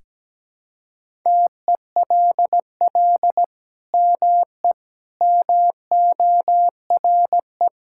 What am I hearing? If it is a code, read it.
Morse code, heard as a keyed tone, about 17 wpm, 700 Hz. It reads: TELL ME MORE
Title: Sentences from Top 1000 Words - Review Part 6 - 17wpm